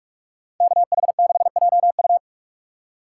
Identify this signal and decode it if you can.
Morse code, heard as a keyed tone, about 56 wpm, 700 Hz. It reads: KH6JU
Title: Speed Racing - USA 2x2 Call Signs 45wpm